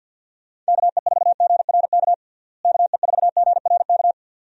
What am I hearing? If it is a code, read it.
Morse code, heard as a keyed tone, about 51 wpm, 700 Hz. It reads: KE4GRK KE4GRK
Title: USA Call Signs - 6 Characters - Repeated 2x - 50wpm